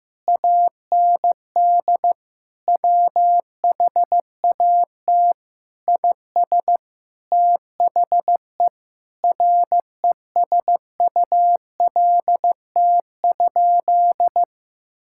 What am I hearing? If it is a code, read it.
Morse code, heard as a keyed tone, about 16 wpm, 700 Hz. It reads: AND WHAT IS THE RESULT?
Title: Sentences from Top 1600 Words 15wpm